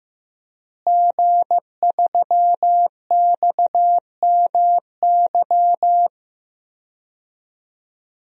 Morse code, heard as a keyed tone, about 15 wpm, 700 Hz. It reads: G3XMY
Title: DXCC-100 Call Signs - 5 Characters - Repeated 2x - 15wpm